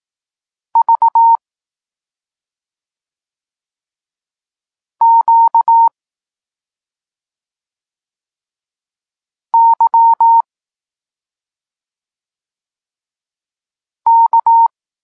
Morse code, heard as a keyed tone, about 18 wpm, 920 Hz. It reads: V Q Y K
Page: Tekst 32 – Doskonalenie odbioru poznanych liter